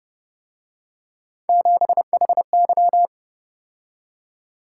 Morse code, heard as a keyed tone, about 31 wpm, 700 Hz. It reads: 7HY